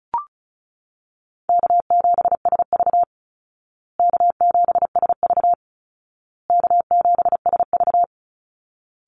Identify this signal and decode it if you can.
Morse code, heard as a keyed tone, about 34 wpm, 700 Hz. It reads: K7SV K7SV K7SV
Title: Easy Call Signs Repeated 3x - 35wpm